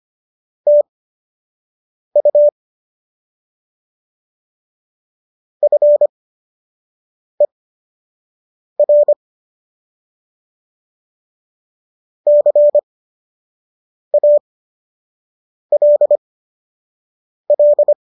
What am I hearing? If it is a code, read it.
Morse code, heard as a keyed tone, about 25 wpm, spaced as for 6 wpm, 590 Hz. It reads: TU FER CALL